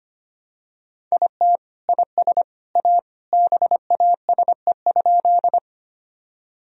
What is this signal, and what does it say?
Morse code, heard as a keyed tone, about 25 wpm, 700 Hz. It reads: IT IS A BASE?
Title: Sentences from Top 1000 Words - Review Part 1 - 25wpm